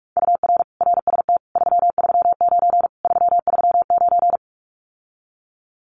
Morse code, heard as a keyed tone, about 46 wpm, 700 Hz. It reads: UR RST 339 339
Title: Common QSO Phrases Encore 45wpm